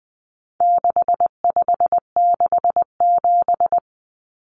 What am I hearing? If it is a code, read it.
Morse code, heard as a keyed tone, about 20 wpm, 700 Hz. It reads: 6567
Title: Numbers - 4-Digit 20wpm